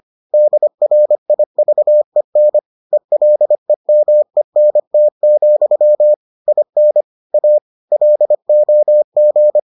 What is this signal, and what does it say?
Morse code, heard as a keyed tone, about 25 wpm, 600 Hz. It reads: DRIVEN ELEMENT, IN A LOG